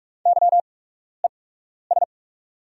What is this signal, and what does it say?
Morse code, heard as a keyed tone, about 44 wpm, spaced as for 12 wpm, 700 Hz. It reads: YES
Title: 8x Character Spacing - Top 500 Words - 45wpm